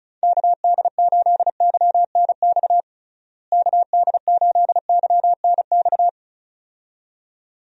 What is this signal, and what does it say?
Morse code, heard as a keyed tone, about 34 wpm, 700 Hz. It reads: KD8YNX KD8YNX
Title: USA Call Signs - 6 Characters - Repeated 2x - 35wpm